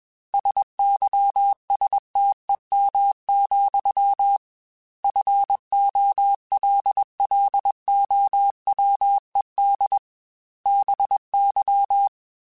Morse code, heard as a keyed tone, about 21 wpm, 800 Hz. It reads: SYSTEM, FOLLOWED BY
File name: Technical Subjects / Technical Subjects-mp3s / Sw Pwr Sup @20